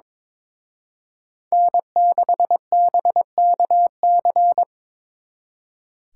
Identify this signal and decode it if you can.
Morse code, heard as a keyed tone, about 26 wpm, 700 Hz. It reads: N6BKC
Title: Speed Racing - USA Call Signs 22wpm